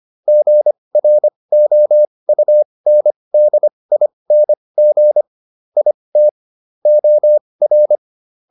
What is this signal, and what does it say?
Morse code, heard as a keyed tone, about 25 wpm, spaced as for 20 wpm, 600 Hz. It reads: GROUNDING IT OR